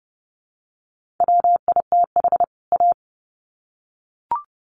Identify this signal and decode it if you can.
Morse code, heard as a keyed tone, about 30 wpm, 700 Hz. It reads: WITH A
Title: Sets of 2 Words 30wpm